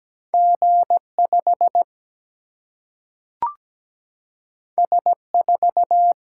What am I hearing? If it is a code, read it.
Morse code, heard as a keyed tone, about 17 wpm, 700 Hz. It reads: G 5 S 4
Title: Single-Letter Single-Number 17wpm